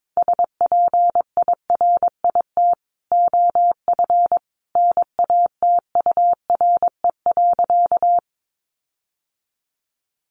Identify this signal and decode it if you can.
Morse code, heard as a keyed tone, about 22 wpm, 700 Hz. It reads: SPIRIT OF NATURE.
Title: Sentences from Top 1400 Words 22wpm